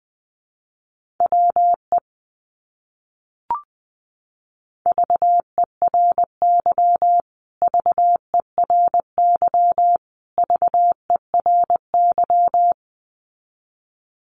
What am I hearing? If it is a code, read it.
Morse code, heard as a keyed tone, about 20 wpm, 700 Hz. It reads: WE VERY VERY VERY
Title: Top 200 Words Repeated 20wpm